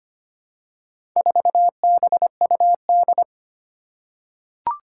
Morse code, heard as a keyed tone, about 25 wpm, 700 Hz. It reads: 4BUD
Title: Four Challenging Letter-Number Combinations 25wpm